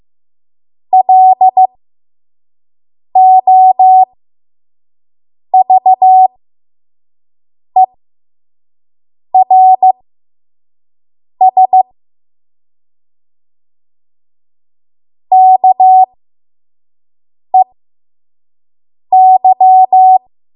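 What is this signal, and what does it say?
Morse code, heard as a keyed tone, about 15 wpm, spaced as for 5 wpm, 750 Hz. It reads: LOVERS KEY
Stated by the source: 5 WPM Morse code practice.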